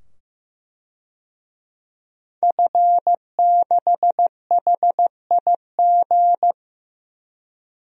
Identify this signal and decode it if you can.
Morse code, heard as a keyed tone, about 15 wpm, 700 Hz. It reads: F6HIG
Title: DXCC-100 Call Signs - 5 Characters - 15wpm